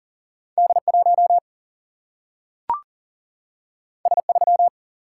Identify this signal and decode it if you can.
Morse code, heard as a keyed tone, about 40 wpm, 700 Hz. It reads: D1 S3
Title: Single-Letter Single-Number 40wpm